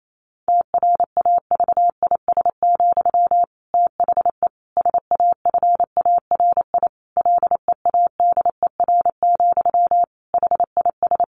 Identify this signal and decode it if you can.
Morse code, heard as a keyed tone, about 28 wpm, 700 Hz. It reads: TRAVIS, THE SAFARI LEADER, HIS
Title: A Sound Of Thunder 28wpm